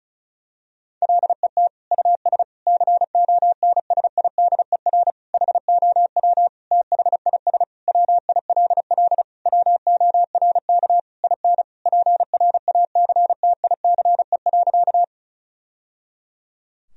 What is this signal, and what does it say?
Morse code, heard as a keyed tone, about 34 wpm, 700 Hz. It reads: LET US CONSIDER HOW THIS WILL WORK IN PRACTICE.
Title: Sentences from Top 1000 Words - Review Part 4 - 35wpm